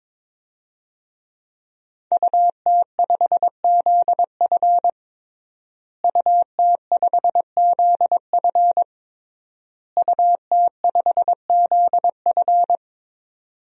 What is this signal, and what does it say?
Morse code, heard as a keyed tone, about 22 wpm, 700 Hz. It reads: UT5ZF UT5ZF UT5ZF
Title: DXCC-100 Call Signs - 5 Characters - Repeated 3x - 22wpm